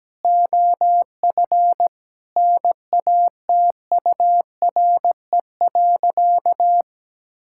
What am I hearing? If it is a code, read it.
Morse code, heard as a keyed tone, about 17 wpm, 700 Hz. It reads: OF NATURE.
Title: Sentences from Top 1000 Words - Review Part 6 - 17wpm